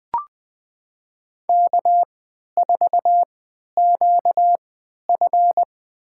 Morse code, heard as a keyed tone, about 20 wpm, spaced as for 11 wpm, 700 Hz. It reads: K4QF
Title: USA Call Signs - 4 Characters - 3x Character Spacing - 20wpm